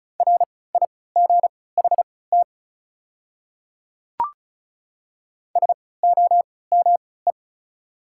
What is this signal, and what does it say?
Morse code, heard as a keyed tone, about 34 wpm, spaced as for 20 wpm, 700 Hz. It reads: RIGHT SOME
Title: 3x Character Spacing - Top 300 Words - 35wpm